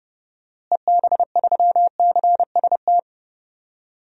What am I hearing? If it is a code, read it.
Morse code, heard as a keyed tone, about 30 wpm, 700 Hz. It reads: EB3CST
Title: DXCC-100 Call Signs - 6 Characters - Repeated 2x - 30wpm